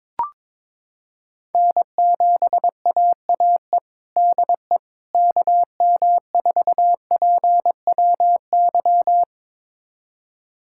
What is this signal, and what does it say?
Morse code, heard as a keyed tone, about 22 wpm, 700 Hz. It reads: N7AAE DE KM4PWY
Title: USA Callsign DE Callsign 22wpm